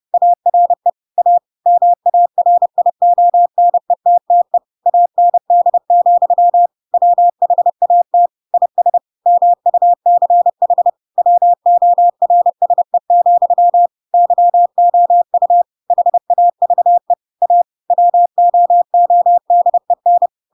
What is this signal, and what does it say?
Morse code, heard as a keyed tone, about 30 wpm, 700 Hz. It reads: ARE A MARIONETTE AND, WHAT IS MUCH WORSE, YOU HAVE A WOODEN
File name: Pinocchio / Pinocchio ch4 30 30 ik1hge